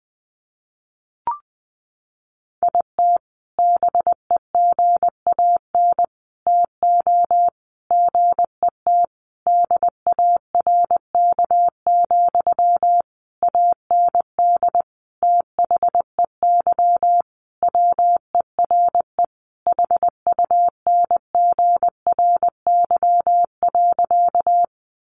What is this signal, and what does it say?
Morse code, heard as a keyed tone, about 20 wpm, 700 Hz. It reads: IT BEGAN TO GET DARK, AND THEY WERE HUNGRY.
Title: Sentences from Top 1400 Words 20wpm